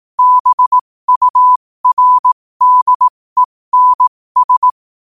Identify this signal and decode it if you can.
Morse code, heard as a keyed tone, about 18 wpm, spaced as for 15 wpm, 1000 Hz. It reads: BURDENS
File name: qotdfiles / 15wpm